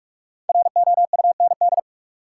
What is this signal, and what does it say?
Morse code, heard as a keyed tone, about 42 wpm, 700 Hz. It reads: ROUND
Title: Sets of 5 Words Encore - New Word Drill - 45wpm